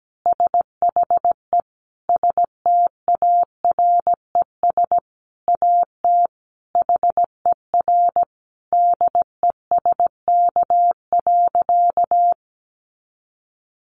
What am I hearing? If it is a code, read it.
Morse code, heard as a keyed tone, about 17 wpm, 700 Hz. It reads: SHE STARES AT HER DESK.
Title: Easy Phrases 17wpm